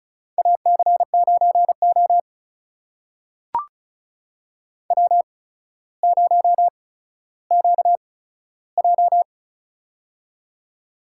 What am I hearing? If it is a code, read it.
Morse code, heard as a keyed tone, about 36 wpm, spaced as for 15 wpm, 700 Hz. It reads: AC9O W0QJ
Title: USA Call Signs - 4 Characters - 8x Character Spacing - 35wpm